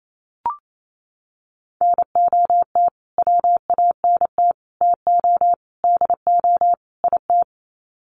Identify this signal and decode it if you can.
Morse code, heard as a keyed tone, about 28 wpm, 700 Hz. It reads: NOT WANT TO DO IT
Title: Sets of 5 Words Encore 28wpm